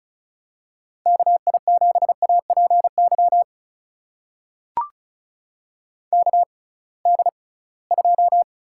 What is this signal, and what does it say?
Morse code, heard as a keyed tone, about 36 wpm, spaced as for 23 wpm, 700 Hz. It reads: KI7APY KD2
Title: USA Call Signs - 6 Characters - 6x Character Spacing - 35wpm